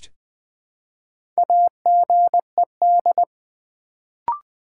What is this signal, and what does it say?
Morse code, heard as a keyed tone, about 20 wpm, 700 Hz. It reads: AGED
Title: Build A Word - 2 Letters At A Time - 4-Letter Words - 20wpm